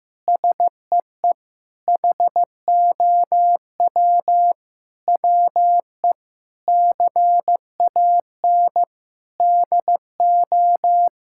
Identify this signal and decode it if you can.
Morse code, heard as a keyed tone, about 15 wpm, 700 Hz. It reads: SEE HOW WE CAN DO
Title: Sentences from Top 1000 Words - Review Part 6 - 15wpm